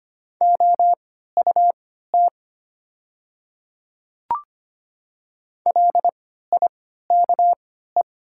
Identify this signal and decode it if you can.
Morse code, heard as a keyed tone, about 25 wpm, spaced as for 14 wpm, 700 Hz. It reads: OUT LIKE
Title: 3x Character Spacing - Top 100 Words - 25wpm